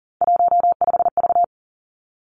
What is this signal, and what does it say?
Morse code, heard as a keyed tone, about 42 wpm, 700 Hz. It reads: 15V
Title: Three Challenging Letter-Number Combinations 40wpm